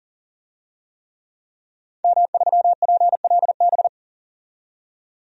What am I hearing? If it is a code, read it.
Morse code, heard as a keyed tone, about 40 wpm, 700 Hz. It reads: M3PLB
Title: DXCC-100 Call Signs - 5 Characters - 40wpm